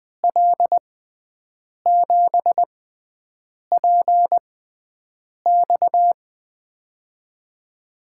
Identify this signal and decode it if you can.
Morse code, heard as a keyed tone, about 20 wpm, spaced as for 7 wpm, 700 Hz. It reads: L7PX
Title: USA Call Signs - 5 Characters - 6x Character Spacing - 20wpm